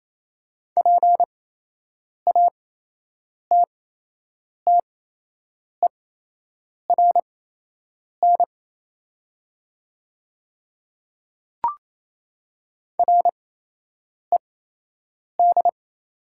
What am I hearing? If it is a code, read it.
Morse code, heard as a keyed tone, about 28 wpm, spaced as for 8 wpm, 700 Hz. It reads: PATTERN RED
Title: 8x Character Spacing - Top 500 Words - 28wpm